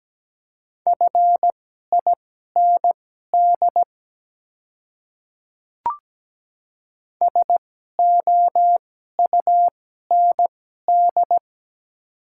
Morse code, heard as a keyed tone, about 17 wpm, spaced as for 12 wpm, 700 Hz. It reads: FIND SOUND
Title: 2x Character Spacing - Top 100 Words - 17wpm